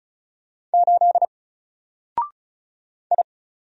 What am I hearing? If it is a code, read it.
Morse code, heard as a keyed tone, about 33 wpm, 700 Hz. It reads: 8 I
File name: Single-Letter-Number-35wpm